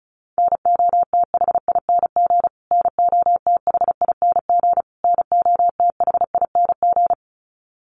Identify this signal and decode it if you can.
Morse code, heard as a keyed tone, about 34 wpm, 700 Hz. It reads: NOTHING NOTHING NOTHING
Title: Top 500 Words Repeated 35wpm